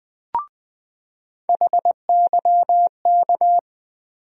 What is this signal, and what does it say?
Morse code, heard as a keyed tone, about 20 wpm, 700 Hz. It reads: HYK
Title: USA Call Sign Postfixes - 3 Letters 20wpm